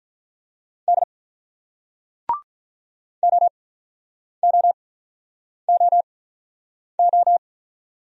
Morse code, heard as a keyed tone, about 48 wpm, spaced as for 9 wpm, 700 Hz. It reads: D OOOO